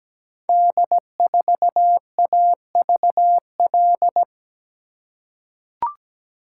Morse code, heard as a keyed tone, about 17 wpm, 700 Hz. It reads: D4AVL
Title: Code Groups - 5 Letter:Numbers - 17wpm